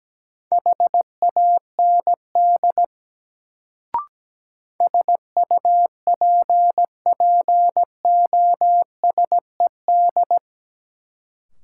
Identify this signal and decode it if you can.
Morse code, heard as a keyed tone, about 17 wpm, 700 Hz. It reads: HAND SUPPOSED
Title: Sets of 3 Words - New Word Drill - 17wpm